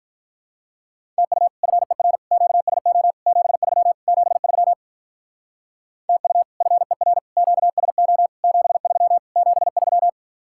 Morse code, heard as a keyed tone, about 44 wpm, 700 Hz. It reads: TU FER QSO 73 73 TU FER QSO 73 73